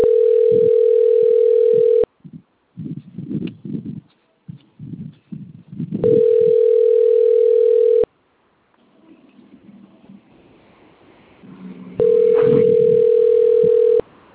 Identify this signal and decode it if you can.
Ringback tone (North America), 440 + 480 Hz.